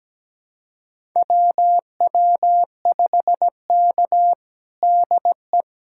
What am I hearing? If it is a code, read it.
Morse code, heard as a keyed tone, about 17 wpm, 700 Hz. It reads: WW5K DE
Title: USA Callsign DE Callsign - 4 Characters - 17wpm